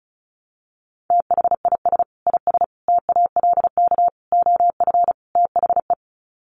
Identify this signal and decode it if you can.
Morse code, heard as a keyed tone, about 34 wpm, 700 Hz. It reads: THIS IS TALK OF THE
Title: Sets of 5 Words 35wpm